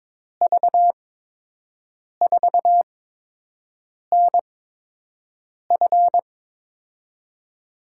Morse code, heard as a keyed tone, about 22 wpm, spaced as for 6 wpm, 700 Hz. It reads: V4NF
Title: USA Call Signs - 5 Characters - 8x Character Spacing - 22wpm